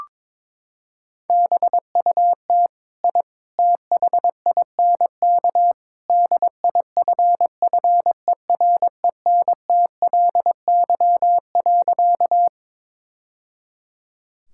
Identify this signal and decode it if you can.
Morse code, heard as a keyed tone, about 22 wpm, 700 Hz. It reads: BUT I THINK DIFFERENTLY.
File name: practice / 22wpm / Sentences with 3-Word Phrases - 22wpm